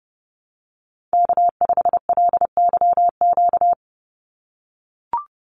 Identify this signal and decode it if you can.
Morse code, heard as a keyed tone, about 30 wpm, 700 Hz. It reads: K5LYQ
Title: USA Call Signs 30wpm